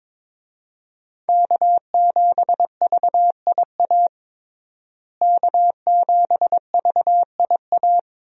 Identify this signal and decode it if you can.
Morse code, heard as a keyed tone, about 21 wpm, 700 Hz. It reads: K7VIA K7VIA
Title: USA Call Signs - 5 Characters - Repeated 2x - 22wpm